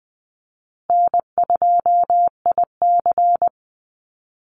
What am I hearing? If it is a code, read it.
Morse code, heard as a keyed tone, about 20 wpm, 700 Hz. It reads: N2IC
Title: Easy Call Signs Repeated 2x - 20wpm